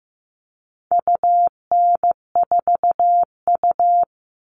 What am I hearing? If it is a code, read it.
Morse code, heard as a keyed tone, about 15 wpm, 700 Hz. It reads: UN4U
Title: All Cut Numbers - 4-Digit 15wpm